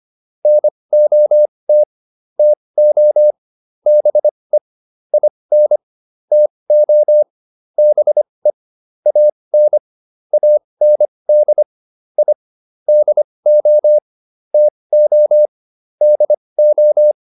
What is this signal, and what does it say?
Morse code, heard as a keyed tone, about 25 wpm, spaced as for 20 wpm, 600 Hz. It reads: NOT TO BE IN TO BE AN AND I DO TO DO